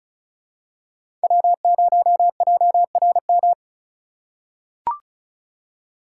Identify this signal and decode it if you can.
Morse code, heard as a keyed tone, about 36 wpm, 700 Hz. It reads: W0JRM
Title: USA Call Signs - 5 Characters - 3x Character Spacing - 35wpm